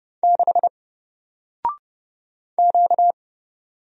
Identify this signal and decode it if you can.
Morse code, heard as a keyed tone, about 30 wpm, 700 Hz. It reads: 6 Q
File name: Single-Letter-Number-30wpm